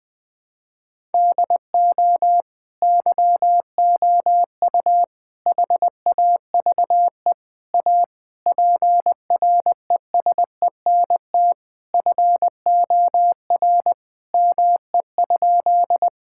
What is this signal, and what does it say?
Morse code, heard as a keyed tone, about 20 wpm, 700 Hz. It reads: DO YOU HAVE A PRESENT FOR ME?
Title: Sentences from Top 1000 Words - Review Part 5 - 20wpm